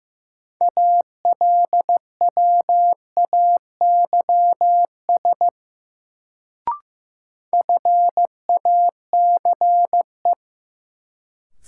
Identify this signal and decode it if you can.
Morse code, heard as a keyed tone, about 15 wpm, 700 Hz. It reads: ALWAYS FACE
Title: Sentences from Top 300 Words - New Word Drill - 15wpm